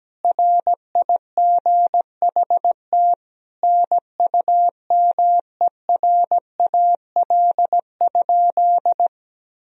Morse code, heard as a keyed tone, about 17 wpm, 700 Hz. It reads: RIGHT NUMERAL?
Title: Sentences from Top 1000 Words - Review Part 2 - 17wpm